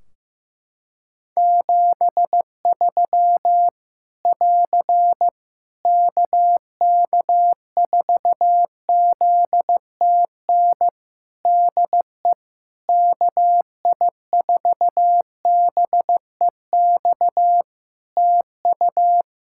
Morse code, heard as a keyed tone, about 15 wpm, 700 Hz. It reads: 73 <AR> KK4ZTN DE KI4BEX TU
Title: K1USN SST QSO Practice 15wpm